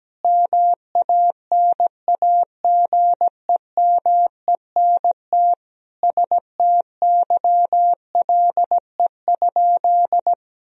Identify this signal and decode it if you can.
Morse code, heard as a keyed tone, about 17 wpm, 700 Hz. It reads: MANAGEMENT STYLE?
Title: Sentences from Top 1000 Words - Review Part 4 - 17wpm